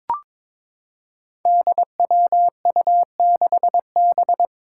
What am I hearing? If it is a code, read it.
Morse code, heard as a keyed tone, about 22 wpm, 700 Hz. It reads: DWU6B
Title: Five Challenging Letter-Number Combinations 22wpm